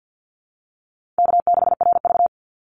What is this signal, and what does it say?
Morse code, heard as a keyed tone, about 49 wpm, 700 Hz. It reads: K6RV
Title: Easy Call Signs Repeated 2x - 50wpm